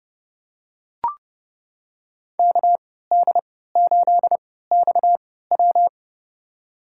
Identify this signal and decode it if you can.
Morse code, heard as a keyed tone, about 31 wpm, spaced as for 17 wpm, 700 Hz. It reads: KD8XW
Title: USA Call Signs - 5 Characters - 3x Character Spacing - 30wpm